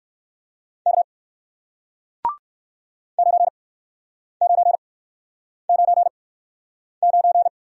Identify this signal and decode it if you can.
Morse code, heard as a keyed tone, about 62 wpm, spaced as for 9 wpm, 700 Hz. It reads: K 9999